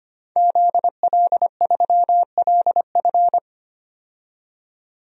Morse code, heard as a keyed tone, about 25 wpm, 700 Hz. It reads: ZL3LF
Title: DXCC-100 Call Signs - 5 Characters - 25wpm